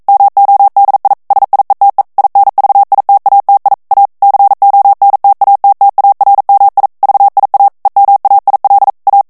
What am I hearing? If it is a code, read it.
Morse code, heard as a keyed tone, about 42 wpm, 800 Hz. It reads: MODI SIETE INVITATI A CONTATTARMI VIA EMAIL A
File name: morse mp3 / ITALIEN texte clair 1 vitesse 40 mots minute